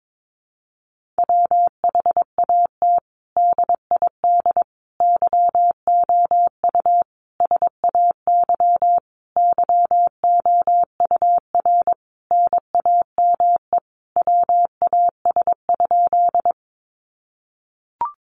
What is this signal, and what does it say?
Morse code, heard as a keyed tone, about 22 wpm, 700 Hz. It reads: WHAT DID YOU SAY YOUR NAME WAS?
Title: Sentences from Top 200 Words 22wpm